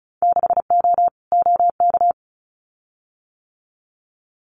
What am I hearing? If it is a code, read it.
Morse code, heard as a keyed tone, about 36 wpm, 700 Hz. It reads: 6O OK
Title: Winter Field Day Exchanges Encore 35wpm